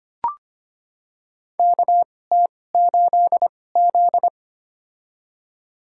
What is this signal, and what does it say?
Morse code, heard as a keyed tone, about 25 wpm, spaced as for 18 wpm, 700 Hz. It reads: KT8Z
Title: USA Call Signs - 4 Characters - 2x Character Spacing - 25wpm